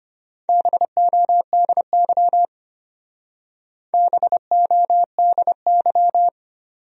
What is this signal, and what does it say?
Morse code, heard as a keyed tone, about 32 wpm, 700 Hz. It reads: BODY BODY